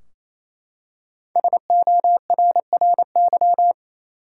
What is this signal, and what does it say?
Morse code, heard as a keyed tone, about 28 wpm, 700 Hz. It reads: SORRY
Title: Five Letter Words - 28wpm